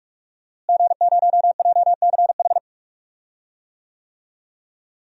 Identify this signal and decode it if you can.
Morse code, heard as a keyed tone, about 46 wpm, 700 Hz. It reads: G0JCH
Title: DXCC-100 Call Signs - 5 Characters - Repeated 2x - 45wpm